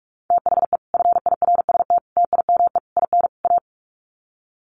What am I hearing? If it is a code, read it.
Morse code, heard as a keyed tone, about 44 wpm, 700 Hz. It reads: THE FIRST TIME IN A
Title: Sets of 5 Words Encore 45wpm